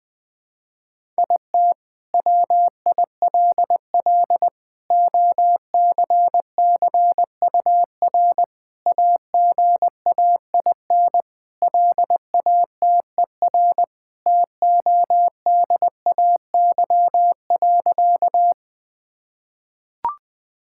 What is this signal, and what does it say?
Morse code, heard as a keyed tone, about 20 wpm, 700 Hz. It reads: IT WILL OCCUR AGAIN LATER TODAY.
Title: Sentences from Top 1000 Words - Review Part 6 - 20wpm